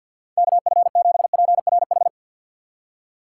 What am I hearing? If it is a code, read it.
Morse code, heard as a keyed tone, about 47 wpm, 700 Hz. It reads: KF7PRH
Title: USA Call Signs - 6 Characters - 4x Character Spacing - 50wpm